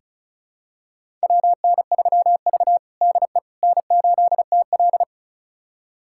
Morse code, heard as a keyed tone, about 36 wpm, 700 Hz. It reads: WN3V DE N8TL
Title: USA Callsign DE Callsign - 4 Characters - 35wpm